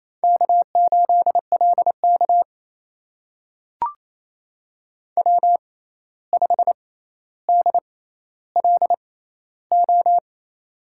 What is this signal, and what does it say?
Morse code, heard as a keyed tone, about 28 wpm, spaced as for 13 wpm, 700 Hz. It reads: K8LK W5DLO
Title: USA Call Signs - 6x Character Spacing - 28wpm